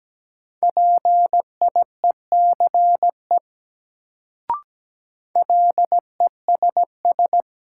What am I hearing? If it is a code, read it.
Morse code, heard as a keyed tone, about 17 wpm, 700 Hz. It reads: PIECE LESS
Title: Sentences from Top 400 Words - New Word Drill - 17wpm